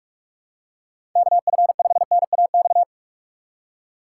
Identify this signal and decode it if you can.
Morse code, heard as a keyed tone, about 46 wpm, 700 Hz. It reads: KF5NAX
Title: USA Call Signs - 6 Characters - 4x Character Spacing - 45wpm